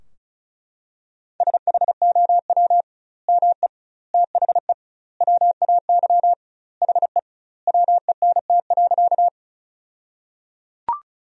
Morse code, heard as a keyed tone, about 34 wpm, 700 Hz. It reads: SHOW ME THE WAY HE WENT.
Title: 2x Word-Spacing - Sentences from Top 200 Words - 35wpm